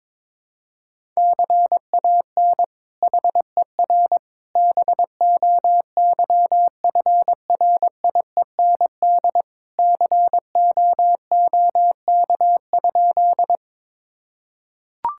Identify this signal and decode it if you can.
Morse code, heard as a keyed tone, about 22 wpm, 700 Hz. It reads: CAN HER BOYFRIEND COOK?
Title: Sentences from Top 1000 Words - Review Part 5 - 22wpm